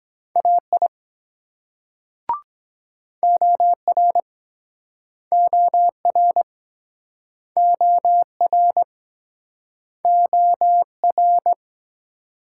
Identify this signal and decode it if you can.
Morse code, heard as a keyed tone, about 20 wpm, 700 Hz. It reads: AI OR OR OR OR
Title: Speed-Racing Multi-letter Phonemes - 17wpm